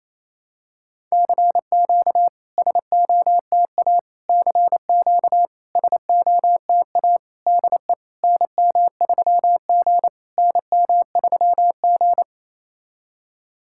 Morse code, heard as a keyed tone, about 28 wpm, 690 Hz. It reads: CQ SOTA CQ SOTA DE NM3G NM3G
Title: SOTA QSO Practice 28wpm